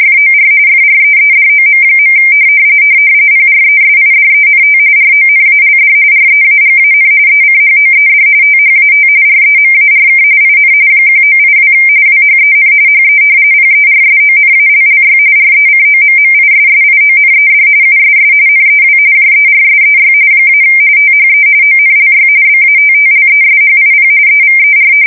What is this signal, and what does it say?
RTTY, about 45.45 baud. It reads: FOR FREEDOM. PRESIDENT RONALD REAGAN ECHOED THAT REMARK IN 1982 WHEN HE PERSONALLY RECOMMISSIONED USS NEW JERSEY - THE ONLY BATTLESHIP EVER
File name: fsk170 / battleshipnj